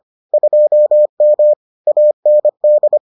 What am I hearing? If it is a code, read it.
Morse code, heard as a keyed tone, about 25 wpm, 600 Hz. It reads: 2M AND